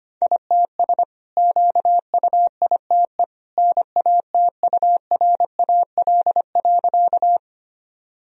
Morse code, heard as a keyed tone, about 25 wpm, 690 Hz. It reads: ITS QUITE NATURAL.
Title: Sentences from Top 1000 Words - Review Part 4 - 25wpm